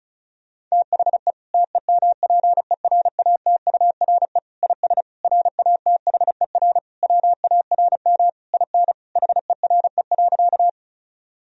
Morse code, heard as a keyed tone, about 34 wpm, 700 Hz. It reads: THE TEMPERATURE IS RATHER WARM IN HERE.
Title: Sentences from Top 1000 Words - Review Part 4 - 35wpm